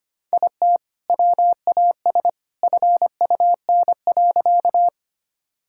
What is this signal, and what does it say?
Morse code, heard as a keyed tone, about 25 wpm, 700 Hz. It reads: IT WAS FUN.
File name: practice / 25wpm / Sentences with 2-Word Phrases - 25wpm